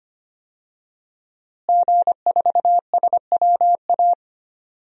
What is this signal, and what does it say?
Morse code, heard as a keyed tone, about 25 wpm, 700 Hz. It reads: G4SWA
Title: DXCC-100 Call Signs - 5 Characters - Repeated 3x - 25wpm